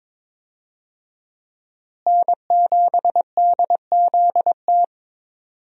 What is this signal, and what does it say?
Morse code, heard as a keyed tone, about 22 wpm, 700 Hz. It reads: N7DZT
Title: USA Call Signs - 5 Characters - Repeated 2x - 22wpm